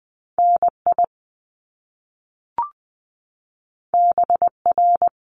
Morse code, heard as a keyed tone, about 20 wpm, 700 Hz. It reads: NI BR
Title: Top Two Letter Combinations in Words 20wpm